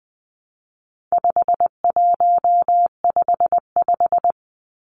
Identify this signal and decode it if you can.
Morse code, heard as a keyed tone, about 20 wpm, 700 Hz. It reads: 5155
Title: Numbers - 4-Digit 20wpm